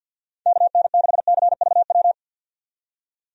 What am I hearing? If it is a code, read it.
Morse code, heard as a keyed tone, about 48 wpm, 700 Hz. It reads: KN6CVW
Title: USA Call Signs - 6 Characters - 50wpm